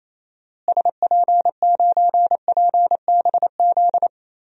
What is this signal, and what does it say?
Morse code, heard as a keyed tone, about 28 wpm, 700 Hz. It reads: SP9PBZ
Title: DXCC-100 Call Signs - 6 Characters - Repeated 2x - 28wpm